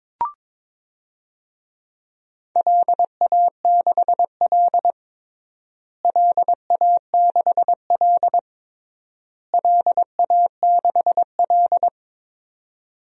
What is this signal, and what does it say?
Morse code, heard as a keyed tone, about 22 wpm, 700 Hz. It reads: LA6L LA6L LA6L
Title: DXCC-100 Call Signs - 4 Characters - Repeated 3x - 22wpm